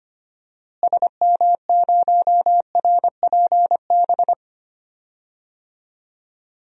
Morse code, heard as a keyed tone, about 25 wpm, 700 Hz. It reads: SM0RPB
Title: DXCC-100 Call Signs - 6 Characters - Repeated 3x - 25wpm